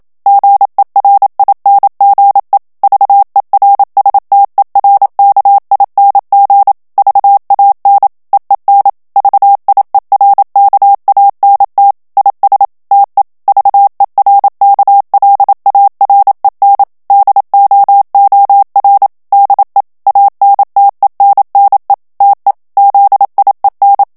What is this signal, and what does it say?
Morse code, heard as a keyed tone, about 27 wpm, 800 Hz. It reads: GERINGE VERSTERKING VAN EEN VIERKANT IS TE VERKLAREN DOOR DE ANTENNE TE ZIEN
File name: HOLLANDAIS texte clair 1 vitesse 26 mots minute